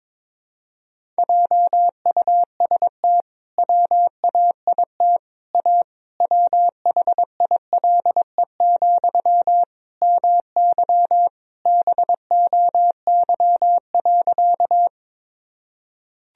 Morse code, heard as a keyed tone, about 22 wpm, 700 Hz. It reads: JUST WAIT A WHILE, MY BOY.
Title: Sentences from Top 1000 Words - Review Part 1 - 22wpm